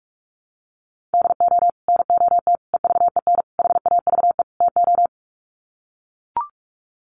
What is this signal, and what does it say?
Morse code, heard as a keyed tone, about 44 wpm, 690 Hz. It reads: DO NOT EVEN HAVE TO
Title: Sets of 5 Words Encore2 45wpm